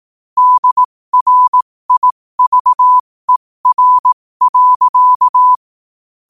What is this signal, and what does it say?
Morse code, heard as a keyed tone, about 18 wpm, spaced as for 15 wpm, 1000 Hz. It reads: DRIVER.
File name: qotdfiles / 15wpm